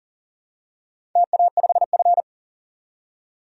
Morse code, heard as a keyed tone, about 40 wpm, 700 Hz. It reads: TA5F
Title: DXCC-100 Call Signs - 4 Characters - 40wpm